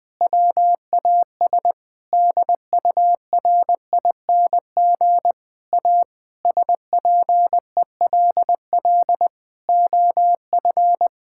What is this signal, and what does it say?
Morse code, heard as a keyed tone, about 20 wpm, 700 Hz. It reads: WAS DURING A SPELL OF
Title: Sentences from Top 1000 Words - Review Part 5 - 20wpm